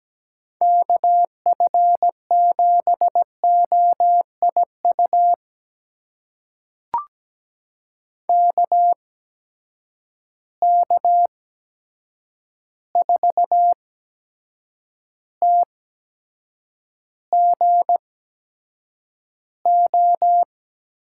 Morse code, heard as a keyed tone, about 17 wpm, spaced as for 7 wpm, 700 Hz. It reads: KF7OIU KK4TGO
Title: USA Call Signs - 6 Characters - 8x Character Spacing - 17wpm